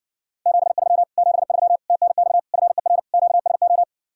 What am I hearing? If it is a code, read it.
Morse code, heard as a keyed tone, about 60 wpm, 700 Hz. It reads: 73 73 TNX FER QSO